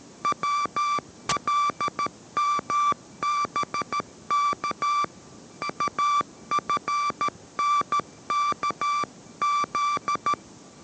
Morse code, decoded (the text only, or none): WLMBK AFNKZ